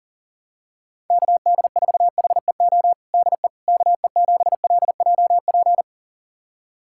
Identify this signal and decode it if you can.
Morse code, heard as a keyed tone, about 40 wpm, 700 Hz. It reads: KD4HEO DE KE7LJP
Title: USA Callsign DE Callsign 40wpm